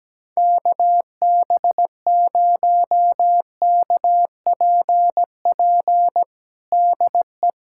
Morse code, heard as a keyed tone, about 17 wpm, 700 Hz. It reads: KB0KPP DE
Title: USA Callsign DE Callsign 17wpm